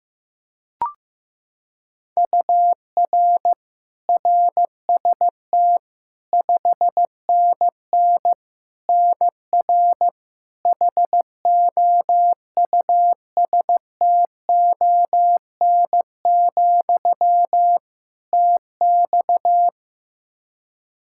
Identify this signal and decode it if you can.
Morse code, heard as a keyed tone, about 15 wpm, 700 Hz. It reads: UR RST 5NN NR HOUSTON, TX
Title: K1USN SST QSO Practice 15wpm